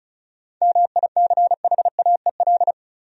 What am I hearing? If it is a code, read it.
Morse code, heard as a keyed tone, about 36 wpm, 700 Hz. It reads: MICHAEL
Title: USA Common Names - 35wpm